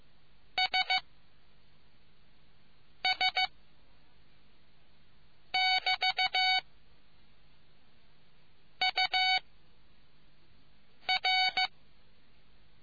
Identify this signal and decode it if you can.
Morse code, heard as a keyed tone, about 15 wpm, 750 Hz. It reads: S S <BT> U R